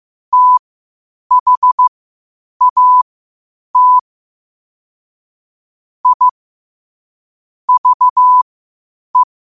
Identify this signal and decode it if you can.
Morse code, heard as a keyed tone, about 15 wpm, spaced as for 8 wpm, 1000 Hz. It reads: THAT I VE
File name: qotdfiles / 8wpm